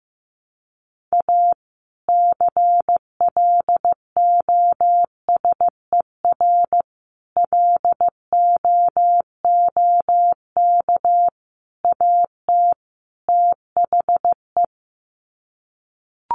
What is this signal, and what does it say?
Morse code, heard as a keyed tone, about 15 wpm, 700 Hz. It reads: A CLOSER LOOK AT THE
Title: Sets of 5 Words Encore 15wpm